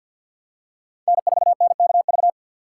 Morse code, heard as a keyed tone, about 52 wpm, 700 Hz. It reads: N4NKV
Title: USA Call Signs - 5 Characters - 2x Character Spacing - 50wpm